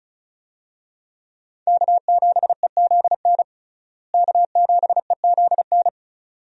Morse code, heard as a keyed tone, about 36 wpm, 700 Hz. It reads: K7EZN K7EZN
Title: USA Call Signs - 5 Characters - Repeated 2x - 35wpm